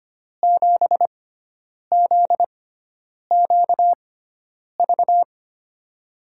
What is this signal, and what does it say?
Morse code, heard as a keyed tone, about 25 wpm, spaced as for 9 wpm, 700 Hz. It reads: 7ZQV
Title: USA Call Signs - 6 Characters - 6x Character Spacing - 25wpm